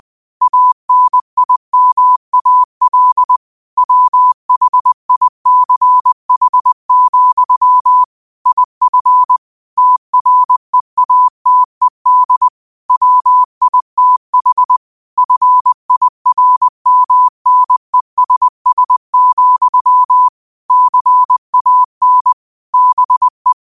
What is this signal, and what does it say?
Morse code, heard as a keyed tone, about 20 wpm, 1000 Hz. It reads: ANIMAL WHICH, IF TREATED WITH FIRMNESS, CAN BE
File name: qotdfiles / 20wpm